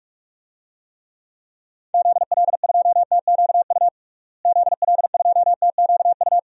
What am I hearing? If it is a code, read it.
Morse code, heard as a keyed tone, about 42 wpm, 700 Hz. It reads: ZL2TQU ZL2TQU
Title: DXCC-100 Call Signs - 6 Characters - Repeated 2x - 45wpm